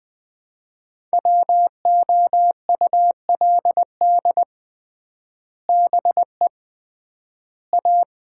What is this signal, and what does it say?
Morse code, heard as a keyed tone, about 20 wpm, 700 Hz. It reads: WOULD BE A
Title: 3x Word-Spacing - Sets of 3 Words - 20wpm